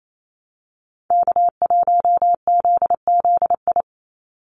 Morse code, heard as a keyed tone, about 28 wpm, 700 Hz. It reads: K1ZZI
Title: Call Signs - 28wpm